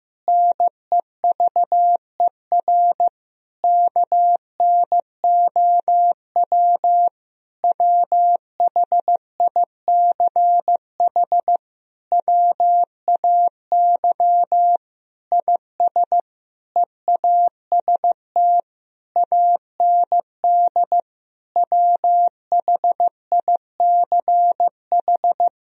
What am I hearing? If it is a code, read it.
Morse code, heard as a keyed tone, about 15 wpm, 700 Hz. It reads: NEVER KNOW WHICH WAY IS EAST AND WHICH
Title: Sentences from Top 1000 Words - Review Part 3 - 15wpm